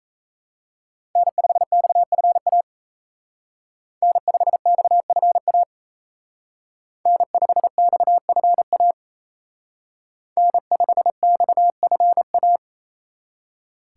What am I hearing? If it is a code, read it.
Morse code, heard as a keyed tone, about 34 wpm, 700 Hz. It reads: N5XFA N5XFA N5XFA N5XFA
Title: Speed Racing - USA Call Signs 28wpm